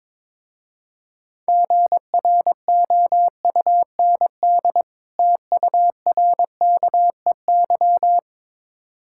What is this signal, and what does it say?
Morse code, heard as a keyed tone, about 22 wpm, 700 Hz. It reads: GROUND TURKEY
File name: Difficult Words with Hints - 22wpm